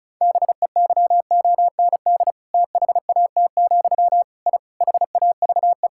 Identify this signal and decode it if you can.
Morse code, heard as a keyed tone, about 36 wpm, 700 Hz. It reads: BEYOND THAT, I HAVE
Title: Sentences from Top 1000 Words - Review Part 6 - 35wpm